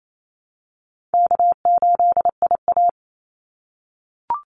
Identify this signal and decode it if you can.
Morse code, heard as a keyed tone, about 28 wpm, 700 Hz. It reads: K8IA
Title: Easy Call Signs - 28wpm